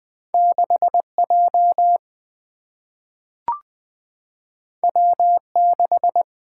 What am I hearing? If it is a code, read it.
Morse code, heard as a keyed tone, about 20 wpm, 700 Hz. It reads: 6J W6
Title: Two Letter-Number Combinations - 20wpm